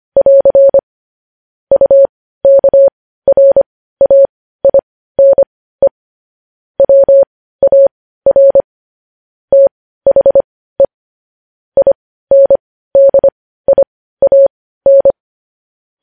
Morse code, heard as a keyed tone, about 25 wpm, spaced as for 15 wpm, 550 Hz. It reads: <AR> UKRAINE WAR THE INDIAN
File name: podcast / News headlines-1x-25x15